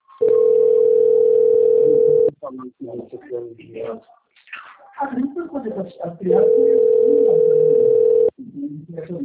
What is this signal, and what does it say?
Ringback tone (North America), 440 + 480 Hz, with speech.
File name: out- - - - - .540